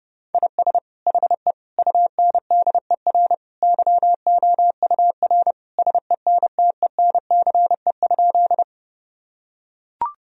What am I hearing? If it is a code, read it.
Morse code, heard as a keyed tone, about 30 wpm, 700 Hz. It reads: IS HE UNDER YOUR SENTENCE?
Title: Sentences from Top 1000 Words - Review Part 1 - 30wpm